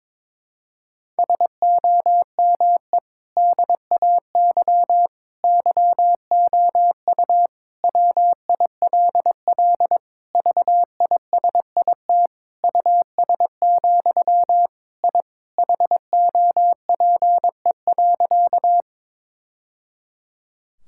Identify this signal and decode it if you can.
Morse code, heard as a keyed tone, about 22 wpm, 700 Hz. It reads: SOME DAY YOU WILL VISIT US, I HOPE.
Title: Sentences from Top 1000 Words - Review Part 6 - 22wpm